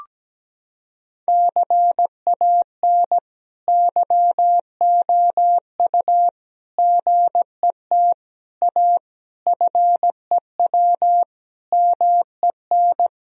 Morse code, heard as a keyed tone, about 17 wpm, 700 Hz. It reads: CAN YOU GET A FEW MEN
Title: Sentences from Top 1000 Words - Review Part 1 - 17wpm